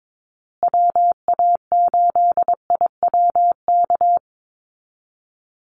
Morse code, heard as a keyed tone, about 22 wpm, 700 Hz. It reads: WA8IWK
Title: Call Signs Repeated 2x - 22wpm